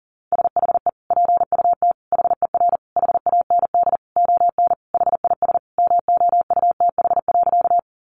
Morse code, heard as a keyed tone, about 40 wpm, 700 Hz. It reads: SHE PUT HER HAND ON HIS MOUTH.
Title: Sentences from Top 1500 Words 40wpm